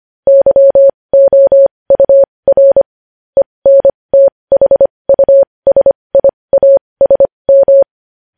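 Morse code, heard as a keyed tone, about 25 wpm, spaced as for 20 wpm, 550 Hz. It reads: YOUR ENTHUSIASM
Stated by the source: News headlines in Morse at 20 wpm.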